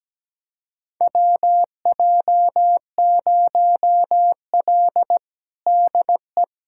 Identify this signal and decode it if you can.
Morse code, heard as a keyed tone, about 17 wpm, 700 Hz. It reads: WJ0L DE
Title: USA Callsign DE Callsign - 4 Characters - 17wpm